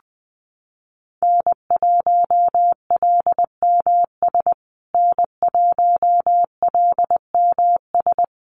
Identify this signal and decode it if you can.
Morse code, heard as a keyed tone, about 20 wpm, 700 Hz. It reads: N1LMS N1LMS
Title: USA Call Signs Repeated 2x 20wpm